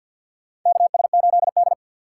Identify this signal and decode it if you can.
Morse code, heard as a keyed tone, about 52 wpm, 700 Hz. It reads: KS8D
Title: USA Call Signs - 4 Characters - 2x Character Spacing - 50wpm